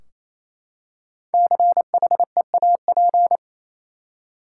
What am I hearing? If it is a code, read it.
Morse code, heard as a keyed tone, about 28 wpm, 700 Hz. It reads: CHEAP
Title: Five Letter Words - 28wpm